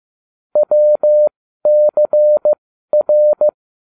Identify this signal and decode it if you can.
Morse code, heard as a keyed tone, about 15 wpm, spaced as for 12 wpm, 600 Hz. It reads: WCR